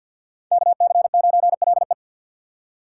Morse code, heard as a keyed tone, about 48 wpm, 700 Hz. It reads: KC9LE
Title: USA Call Signs - 5 Characters - 6x Character Spacing - 50wpm